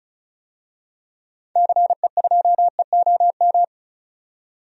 Morse code, heard as a keyed tone, about 35 wpm, 700 Hz. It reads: CE2EOM
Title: DXCC-100 Call Signs - 6 Characters - 35wpm